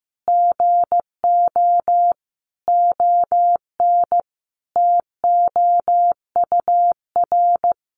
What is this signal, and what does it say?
Morse code, heard as a keyed tone, about 15 wpm, 700 Hz. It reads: GO ON TOUR
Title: Sentences from Top 1400 Words 15wpm